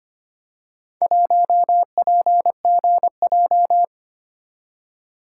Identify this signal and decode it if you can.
Morse code, heard as a keyed tone, about 25 wpm, 700 Hz. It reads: 1PGJ
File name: practice / 25wpm / Four Letter-Number Combinations - 25wpm